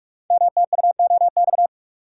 Morse code, heard as a keyed tone, about 44 wpm, 690 Hz. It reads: MTUOX
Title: Code Groups - 5 Letters - 45wpm